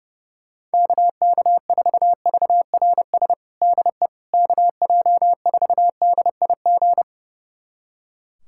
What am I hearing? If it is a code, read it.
Morse code, heard as a keyed tone, about 30 wpm, 700 Hz. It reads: KK4VRS DE KJ4DIG
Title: USA Callsign DE Callsign - 6 Characters - 30wpm